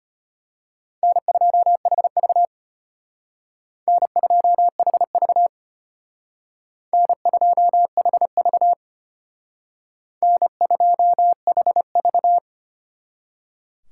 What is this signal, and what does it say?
Morse code, heard as a keyed tone, about 33 wpm, 700 Hz. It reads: N2HV N2HV N2HV N2HV